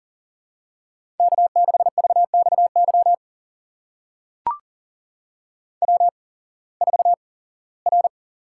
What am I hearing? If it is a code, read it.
Morse code, heard as a keyed tone, about 40 wpm, spaced as for 21 wpm, 700 Hz. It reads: K6VXY W4R
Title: USA Call Signs - 5 Characters - 8x Character Spacing - 40wpm